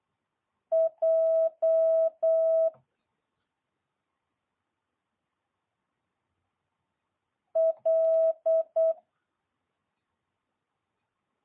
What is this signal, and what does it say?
Morse code, heard as a keyed tone, about 8 wpm, 650 Hz. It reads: J L
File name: jh8cbh